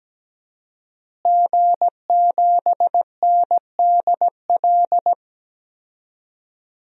Morse code, heard as a keyed tone, about 17 wpm, 700 Hz. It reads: G7NDL
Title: DXCC-100 Call Signs - 5 Characters - Repeated 3x - 17wpm